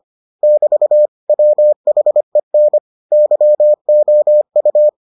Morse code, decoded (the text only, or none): <BT> WHEN YOU